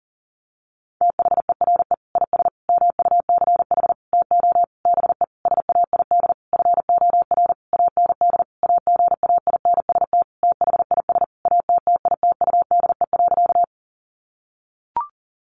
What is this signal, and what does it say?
Morse code, heard as a keyed tone, about 40 wpm, 700 Hz. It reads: THERE IS MUCH TO BE SAID FOR AND AGAINST THIS ATTITUDE.
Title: Sentences from Top 1300 Words 40wpm